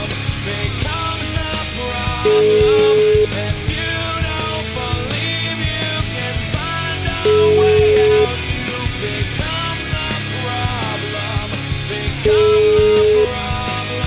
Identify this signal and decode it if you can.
Ringback tone (Europe, 425 Hz), with noise.